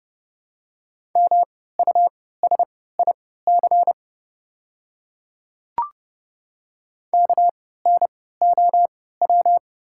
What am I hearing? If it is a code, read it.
Morse code, heard as a keyed tone, about 29 wpm, spaced as for 17 wpm, 690 Hz. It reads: MUSIC KNOW
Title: 3x Character Spacing - Top 300 Words - 30wpm